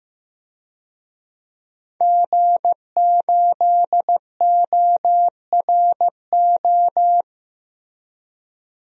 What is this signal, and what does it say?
Morse code, heard as a keyed tone, about 15 wpm, 700 Hz. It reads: G8ORO
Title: DXCC-100 Call Signs - 5 Characters - Repeated 3x - 15wpm